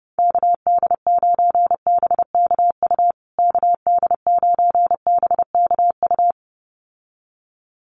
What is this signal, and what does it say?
Morse code, heard as a keyed tone, about 29 wpm, 700 Hz. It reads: KD9BKU KD9BKU
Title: USA Call Signs Repeated 2x 30wpm